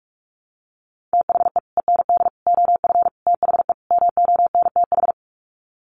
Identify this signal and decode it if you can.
Morse code, heard as a keyed tone, about 44 wpm, 700 Hz. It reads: THE END OF THE MONTH
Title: Sets of 5 Words Encore 45wpm